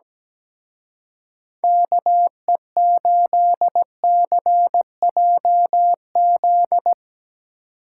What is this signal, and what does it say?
Morse code, heard as a keyed tone, about 19 wpm, 700 Hz. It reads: KE8CJZ
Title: Speed Racing - USA 2x3 Call Signs 17wpm